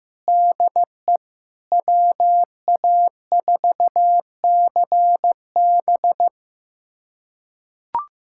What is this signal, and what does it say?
Morse code, heard as a keyed tone, about 15 wpm, 700 Hz. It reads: DE WA4CB
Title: USA Callsign DE Callsign - 5 Characters - 15wpm